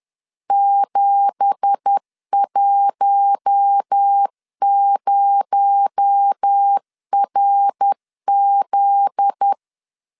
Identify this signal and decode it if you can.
Morse code, heard as a keyed tone, about 10 wpm, 800 Hz. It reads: 710RZ